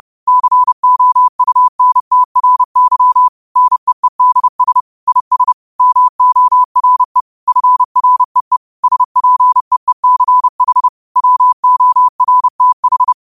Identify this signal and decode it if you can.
Morse code, heard as a keyed tone, about 30 wpm, 1000 Hz. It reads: COUNTRY NEEDS IS MORE FREE SPEECH WORTH